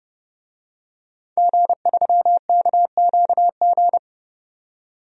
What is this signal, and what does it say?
Morse code, heard as a keyed tone, about 30 wpm, 700 Hz. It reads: G3KQG
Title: DXCC-100 Call Signs - 5 Characters - 30wpm